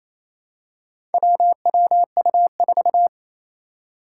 Morse code, heard as a keyed tone, about 28 wpm, 700 Hz. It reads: WWU4
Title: Four Challenging Letter-Number Combinations 28wpm